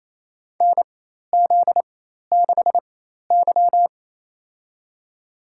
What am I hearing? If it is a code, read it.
Morse code, heard as a keyed tone, about 28 wpm, spaced as for 13 wpm, 700 Hz. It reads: NZ6Y
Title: USA Call Signs - 4 Characters - 4x Character Spacing - 28wpm